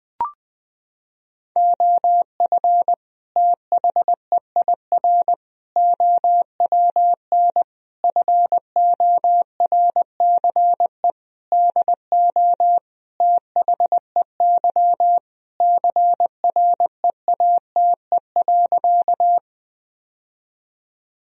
Morse code, heard as a keyed tone, about 21 wpm, 700 Hz. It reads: OF THEIR OWN FORCE DO THEY CREATE.
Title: Sentences from Top 1000 Words - Review Part 3 - 20wpm